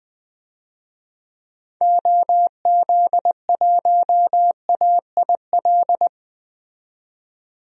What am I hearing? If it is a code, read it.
Morse code, heard as a keyed tone, about 20 wpm, 700 Hz. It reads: OZ1AIL
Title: DXCC-100 Call Signs - 6 Characters - 20wpm